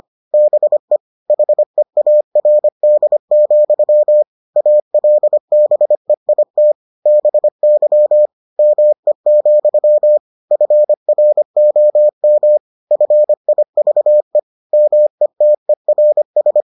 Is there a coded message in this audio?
Yes: Morse code from 0.3 to 16.6 s, heard as a keyed tone, about 25 wpm, 600 Hz. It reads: BE HEARD, ALBEIT BY ME, FROM FIVE METERS